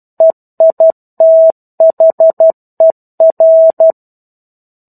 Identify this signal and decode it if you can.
Morse code, heard as a keyed tone, about 12 wpm, 650 Hz. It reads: EITHER
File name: 01-Chapter-01 12 WPM